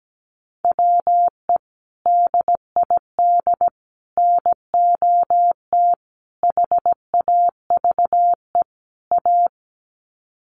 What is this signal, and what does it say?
Morse code, heard as a keyed tone, about 17 wpm, 700 Hz. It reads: WE DID NOT HAVE A
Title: Sets of 5 Words Encore2 17wpm